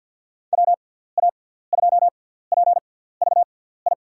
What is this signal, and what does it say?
Morse code, heard as a keyed tone, about 53 wpm, spaced as for 17 wpm, 700 Hz. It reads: WA2PVI
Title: USA Call Signs - 6x Character Spacing - 50wpm